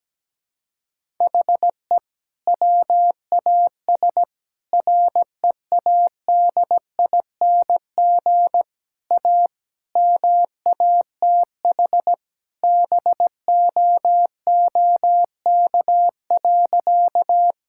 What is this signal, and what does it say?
Morse code, heard as a keyed tone, about 17 wpm, 700 Hz. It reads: HE WAS READING A MATH BOOK.
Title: Sentences from Top 1000 Words - Review Part 6 - 17wpm